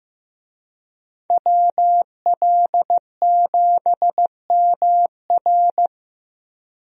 Morse code, heard as a keyed tone, about 15 wpm, 700 Hz. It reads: WL7MR
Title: USA Call Signs - 5 Characters - Repeated 3x - 15wpm